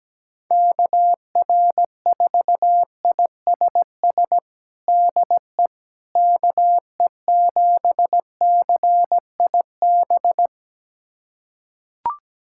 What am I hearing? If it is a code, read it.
Morse code, heard as a keyed tone, about 17 wpm, 700 Hz. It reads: KR4ISS DE KE7CIB
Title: USA Callsign DE Callsign - 6 Characters - 17wpm